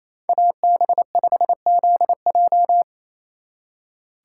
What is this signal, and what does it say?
Morse code, heard as a keyed tone, about 33 wpm, 700 Hz. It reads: AB5ZJ
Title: Speed Racing - USA 2x2 Call Signs 28wpm